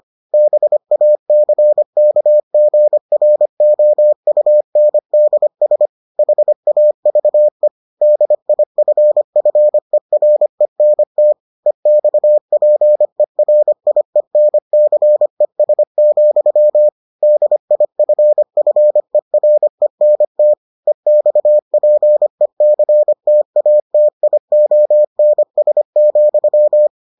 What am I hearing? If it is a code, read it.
Morse code, heard as a keyed tone, about 25 wpm, 600 Hz. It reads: BACKGROUNDS HAVE DIFFERENT EXPERIENCES, DIFFERENT EXPECTATIONS,